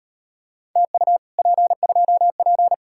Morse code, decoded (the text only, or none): TU P2P